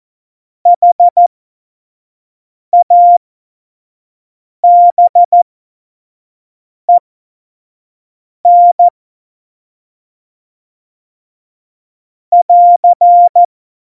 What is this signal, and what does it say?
Morse code, heard as a keyed tone, about 14 wpm, spaced as for 5 wpm, 700 Hz. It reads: HABEN <AR>
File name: Morsekurs / Witze / Witz 14. Januar 2020, 25 F70